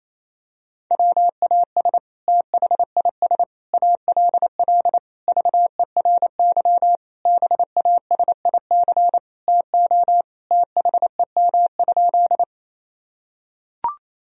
Morse code, heard as a keyed tone, about 27 wpm, 700 Hz. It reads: WAS THIS ALL VERY BASIC TO THEM?
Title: Sentences from Top 1000 Words - Review Part 3 - 28wpm